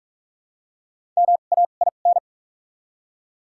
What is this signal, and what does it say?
Morse code, heard as a keyed tone, about 46 wpm, spaced as for 33 wpm, 700 Hz. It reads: MAIN
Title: 2x Character Spacing - Top 500 Words - 45wpm